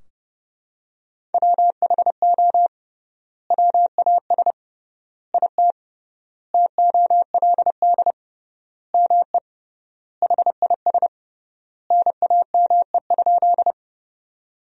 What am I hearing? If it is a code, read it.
Morse code, heard as a keyed tone, about 30 wpm, 700 Hz. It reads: WHO WAS IT TOLD ME HIS NAME?
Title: 3x Word-Spacing - Sentences from Top 400 Words - 30wpm